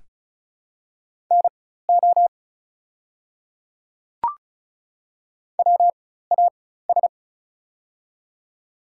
Morse code, heard as a keyed tone, about 34 wpm, spaced as for 16 wpm, 690 Hz. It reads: NO WAS
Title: 4x Character Spacing - Top 100 Words - 35wpm